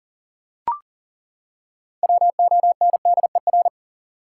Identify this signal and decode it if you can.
Morse code, heard as a keyed tone, about 40 wpm, 700 Hz. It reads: WONDER
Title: Six Letter Words - 40wpm